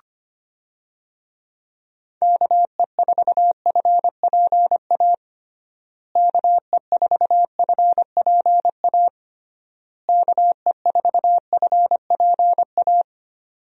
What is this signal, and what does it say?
Morse code, heard as a keyed tone, about 25 wpm, 700 Hz. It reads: KE4FPA KE4FPA KE4FPA
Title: USA Call Signs - 6 Characters - Repeated 3x - 25wpm